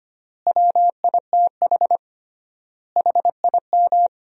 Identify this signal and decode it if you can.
Morse code, heard as a keyed tone, about 25 wpm, 700 Hz. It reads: WITH HIM
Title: 3x Word-Spacing - Sentences from Top 200 Words - 25wpm